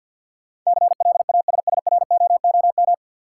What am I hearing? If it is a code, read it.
Morse code, heard as a keyed tone, about 52 wpm, 700 Hz. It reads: CLASSROOM
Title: Sentences from Top 1500 Words - New Word Drill Alt - 50wpm